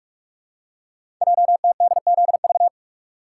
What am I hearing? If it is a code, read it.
Morse code, heard as a keyed tone, about 44 wpm, 690 Hz. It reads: JTDZV
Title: Code Groups - 5 Letters - 45wpm